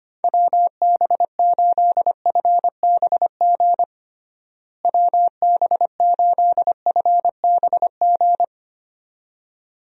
Morse code, heard as a keyed tone, about 25 wpm, 700 Hz. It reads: WB8FBG WB8FBG
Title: USA Call Signs - 6 Characters - Repeated 2x - 25wpm